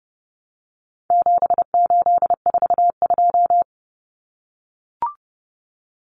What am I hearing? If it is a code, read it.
Morse code, heard as a keyed tone, about 30 wpm, 700 Hz. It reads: 7842
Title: Numbers - 4-Digit 30wpm